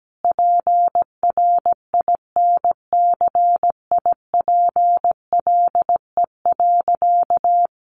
Morse code, heard as a keyed tone, about 17 wpm, 700 Hz. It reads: PRINCIPLE.
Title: Sentences from Top 1400 Words 17wpm